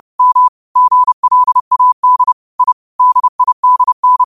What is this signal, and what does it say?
Morse code, heard as a keyed tone, about 30 wpm, 1000 Hz. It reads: M GLAD I DIDN
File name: qotdfiles / 30wpm